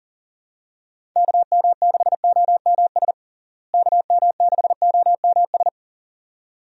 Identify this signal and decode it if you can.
Morse code, heard as a keyed tone, about 40 wpm, 700 Hz. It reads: KM6OMS KM6OMS
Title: USA Call Signs - 6 Characters - Repeated 2x - 40wpm